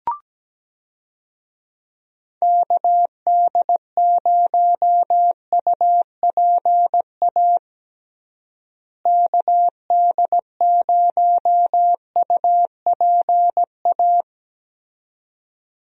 Morse code, heard as a keyed tone, about 17 wpm, 700 Hz. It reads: KD0UPA KD0UPA
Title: USA Call Signs - 6 Characters - Repeated 2x - 17wpm